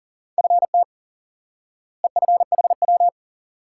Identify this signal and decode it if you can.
Morse code, heard as a keyed tone, about 40 wpm, 700 Hz. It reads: FT EFHW
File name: course / instant-qso-element-course-3 / IQE Course 3 - Lesson 89 - YAGI - Two Elements - 40wpm